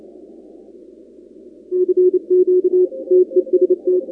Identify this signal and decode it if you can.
Morse code, heard as a keyed tone, about 29 wpm, 360 Hz. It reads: CQ TEST
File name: contests / 2019-cw-wpx-kt5j-k5pi- -2330 0-37